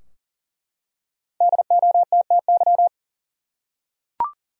Morse code, heard as a keyed tone, about 39 wpm, 700 Hz. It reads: DOTTY
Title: USA Common Names - 40wpm